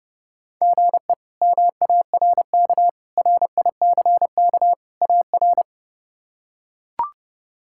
Morse code, heard as a keyed tone, about 30 wpm, 700 Hz. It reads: GE MARK RICK AR
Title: K1USN SST QSO Practice - 30wpm